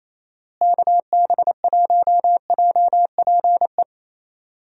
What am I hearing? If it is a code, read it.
Morse code, heard as a keyed tone, about 27 wpm, 700 Hz. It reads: KB1JPE
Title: USA Call Signs - 6 Characters - Repeated 3x - 28wpm